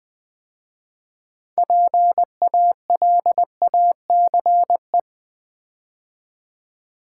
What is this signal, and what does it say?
Morse code, heard as a keyed tone, about 20 wpm, 700 Hz. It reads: PALACE
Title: 6 Letter Words - First 3 Letters Given - 20wpm